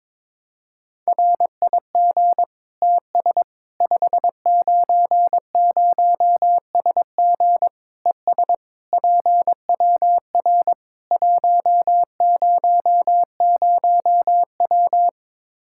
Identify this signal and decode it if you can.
Morse code, heard as a keyed tone, about 22 wpm, 700 Hz. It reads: RIG TS 590SG ES PWR 100W
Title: K1USN SST QSO Practice 22wpm